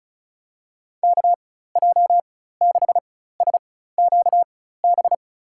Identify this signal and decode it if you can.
Morse code, heard as a keyed tone, about 36 wpm, spaced as for 16 wpm, 700 Hz. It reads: KJ6SQB
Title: USA Call Signs - 6 Characters - 4x Character Spacing - 35wpm